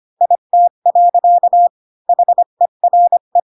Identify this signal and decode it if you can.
Morse code, heard as a keyed tone, about 25 wpm, 700 Hz. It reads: IT. HERE
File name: Pinocchio / Pinocchio ch1 20 25 ik1hge